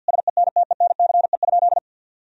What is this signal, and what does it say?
Morse code, heard as a keyed tone, about 49 wpm, 700 Hz. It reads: SENTENCE?
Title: Sentences from Top 1000 Words - Review Part 1 - 50wpm